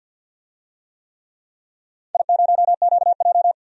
Morse code, heard as a keyed tone, about 52 wpm, 700 Hz. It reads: I0QJ
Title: DXCC-100 Call Signs - 4 Characters - 50wpm